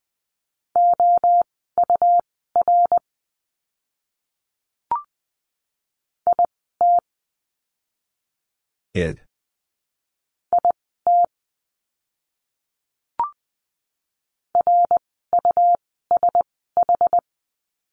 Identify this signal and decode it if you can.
Morse code, heard as a keyed tone, about 20 wpm, spaced as for 14 wpm, 700 Hz. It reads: OUR IT IT RUSH
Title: 101 Lesson 17f - Letter u - Words 20wpm